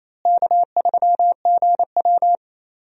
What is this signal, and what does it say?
Morse code, heard as a keyed tone, about 27 wpm, 700 Hz. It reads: K3GW
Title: USA Call Signs - 4 Characters - Repeated 3x - 28wpm